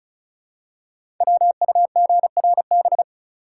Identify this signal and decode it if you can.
Morse code, heard as a keyed tone, about 36 wpm, 700 Hz. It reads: WUGRB
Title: Code Groups - 5 Letters - 35wpm